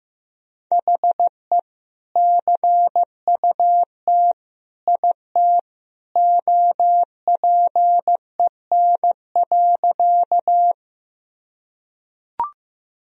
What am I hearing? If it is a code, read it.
Morse code, heard as a keyed tone, about 15 wpm, 700 Hz. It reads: HE CUT IT OPEN.
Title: Sentences from Top 1000 Words - Review Part 1 - 15wpm